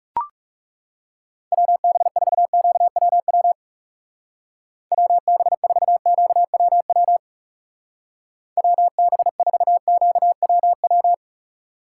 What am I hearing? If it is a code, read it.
Morse code, heard as a keyed tone, about 35 wpm, 700 Hz. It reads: WB4QWW WB4QWW WB4QWW
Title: Speed Racing - USA 2x3 Call Signs 30wpm